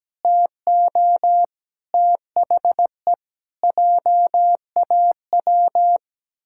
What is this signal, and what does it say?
Morse code, heard as a keyed tone, about 17 wpm, 700 Hz. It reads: TO THE JAW
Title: Pangram Challenge v2 - 17wpm